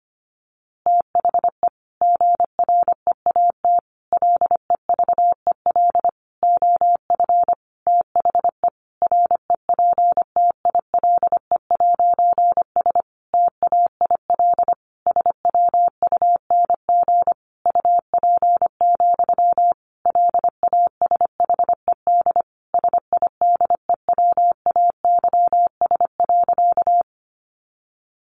Morse code, heard as a keyed tone, about 25 wpm, 700 Hz. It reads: THE GREAT LEVEL OF THE REPTILE'S TAIL SWUNG UP, LASHED SIDEWAYS.
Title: A Sound Of Thunder 25wpm